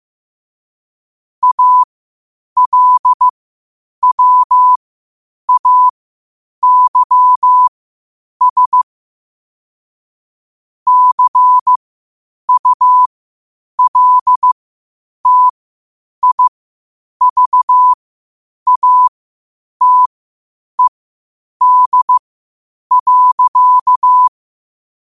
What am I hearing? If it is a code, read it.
Morse code, heard as a keyed tone, about 15 wpm, spaced as for 8 wpm, 1000 Hz. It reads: ALWAYS CULTIVATED.
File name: qotdfiles / 8wpm